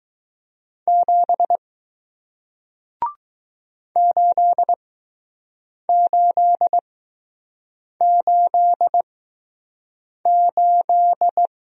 Morse code, heard as a keyed tone, about 20 wpm, spaced as for 6 wpm, 700 Hz. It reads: 7 8888